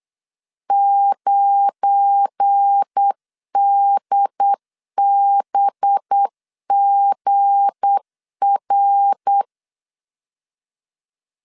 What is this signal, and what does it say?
Morse code, heard as a keyed tone, about 8 wpm, 800 Hz. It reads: 9DBGR
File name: LEZIONI / 7 GF9Z / 7 GF9 8wpm 3545